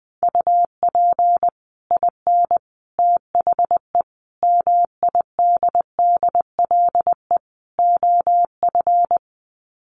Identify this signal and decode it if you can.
Morse code, heard as a keyed tone, about 20 wpm, 700 Hz. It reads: UP IN THE MIDDLE OF
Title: Sets of 5 Words Encore 20wpm